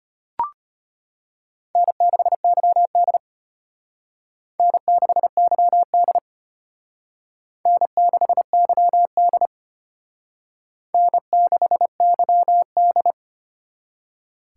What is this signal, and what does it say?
Morse code, heard as a keyed tone, about 31 wpm, 700 Hz. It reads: N6YD N6YD N6YD N6YD
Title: Speed Racing - USA 1x2 Call Signs 25wpm